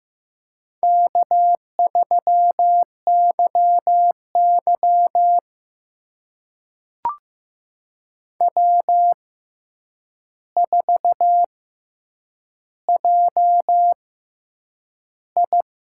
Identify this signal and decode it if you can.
Morse code, heard as a keyed tone, about 15 wpm, spaced as for 8 wpm, 700 Hz. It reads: K3YY W4JI
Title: USA Call Signs - 4 Characters - 6x Character Spacing - 15wpm